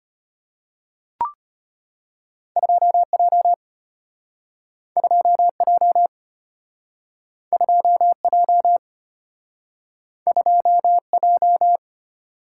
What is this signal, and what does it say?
Morse code, heard as a keyed tone, about 34 wpm, 700 Hz. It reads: 2J 2J 2J 2J